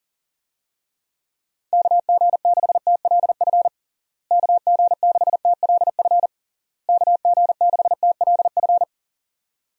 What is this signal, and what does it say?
Morse code, heard as a keyed tone, about 40 wpm, 700 Hz. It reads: KG6TLF KG6TLF KG6TLF
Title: USA Call Signs - 6 Characters - Repeated 3x - 40wpm